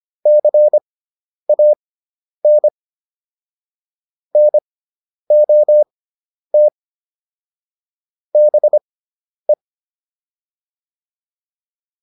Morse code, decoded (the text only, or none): CAN NOT BE